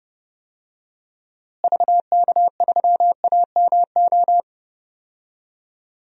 Morse code, heard as a keyed tone, about 30 wpm, 700 Hz. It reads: VK3AMO
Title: DXCC-100 Call Signs - 6 Characters - Repeated 3x - 30wpm